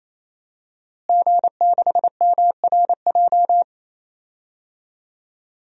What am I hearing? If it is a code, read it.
Morse code, heard as a keyed tone, about 27 wpm, 700 Hz. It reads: G6MRJ
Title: DXCC-100 Call Signs - 5 Characters - Repeated 3x - 28wpm